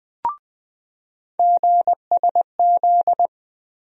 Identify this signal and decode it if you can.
Morse code, heard as a keyed tone, about 20 wpm, 700 Hz. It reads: GSZ
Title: USA Call Sign Postfixes - 3 Letters 20wpm